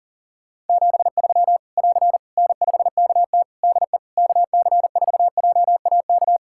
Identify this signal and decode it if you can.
Morse code, heard as a keyed tone, about 40 wpm, 700 Hz. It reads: 73 <AR> N5KT DE KC4JAK
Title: K1USN SST QSO Practice 40wpm